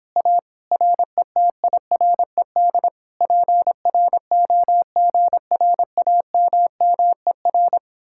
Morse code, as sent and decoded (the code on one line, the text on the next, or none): .- / .-. - .. .-. -.. / .--. .-. --- --. .-. .- -- -- . .-.
A RETIRED PROGRAMMER